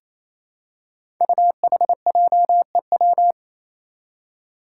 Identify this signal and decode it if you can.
Morse code, heard as a keyed tone, about 28 wpm, 700 Hz. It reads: UHJEW
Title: Five Challenging Letter-Number Combinations 28wpm